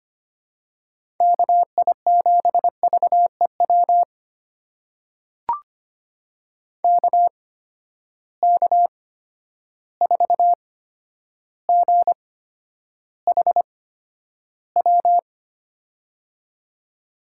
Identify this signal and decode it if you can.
Morse code, heard as a keyed tone, about 25 wpm, spaced as for 11 wpm, 700 Hz. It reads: KI7VEW KK4GHW
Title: USA Call Signs - 6 Characters - 8x Character Spacing - 25wpm